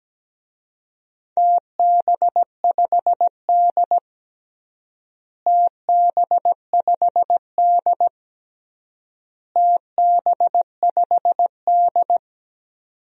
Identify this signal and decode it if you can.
Morse code, heard as a keyed tone, about 17 wpm, 700 Hz. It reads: TB5D TB5D TB5D
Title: DXCC-100 Call Signs - 4 Characters - Repeated 3x - 17wpm